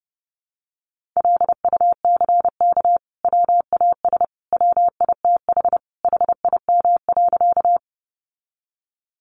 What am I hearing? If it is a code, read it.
Morse code, heard as a keyed tone, about 30 wpm, 700 Hz. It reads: LUCK WAS WITH HIM.
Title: Sentences from Top 1500 Words 30wpm